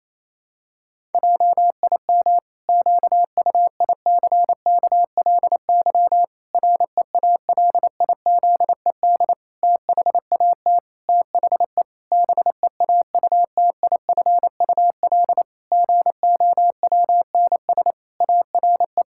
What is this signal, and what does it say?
Morse code, heard as a keyed tone, about 28 wpm, 700 Hz. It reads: JIM QUICKLY REALIZED THAT THE BEAUTIFUL GOWNS ARE
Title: Pangram Challenge v2 - 28wpm